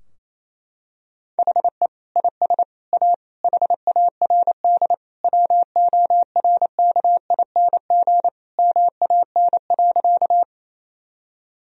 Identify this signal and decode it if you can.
Morse code, heard as a keyed tone, about 28 wpm, 700 Hz. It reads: HE IS A HARD WORKING MAN.
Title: Sentences from Top 1000 Words - Review Part 4 - 28wpm